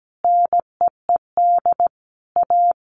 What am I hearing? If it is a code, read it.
Morse code, heard as a keyed tone, about 17 wpm, 700 Hz. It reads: NEED A
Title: Sentences from Top 200 Words 17wpm